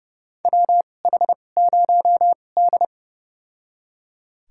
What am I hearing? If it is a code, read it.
Morse code, heard as a keyed tone, about 29 wpm, spaced as for 21 wpm, 700 Hz. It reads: WH0D
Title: USA Call Signs - 4 Characters - 2x Character Spacing - 30wpm